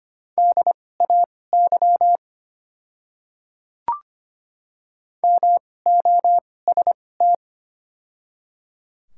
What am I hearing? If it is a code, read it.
Morse code, heard as a keyed tone, about 25 wpm, spaced as for 18 wpm, 700 Hz. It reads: DAY MOST
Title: 2x Character Spacing - Top 100 Words - 25wpm